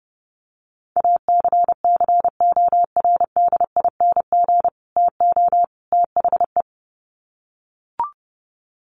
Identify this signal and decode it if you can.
Morse code, heard as a keyed tone, about 30 wpm, 700 Hz. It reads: ACCORDING TO THE
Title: Sets of 3 Words 30wpm